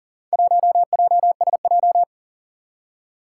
Morse code, heard as a keyed tone, about 40 wpm, 700 Hz. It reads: 1JSJ
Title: Four Challenging Letter-Number Combinations 40wpm